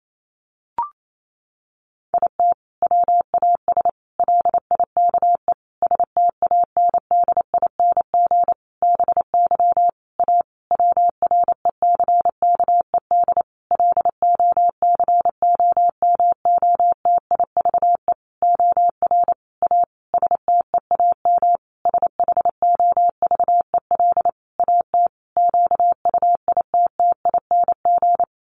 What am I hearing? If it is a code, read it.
Morse code, heard as a keyed tone, about 28 wpm, 700 Hz. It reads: IT WAS LIKE STANDING BY A WRECKED LOCOMOTIVE OR A STEAM SHOVEL AT QUITTING
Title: A Sound Of Thunder 28wpm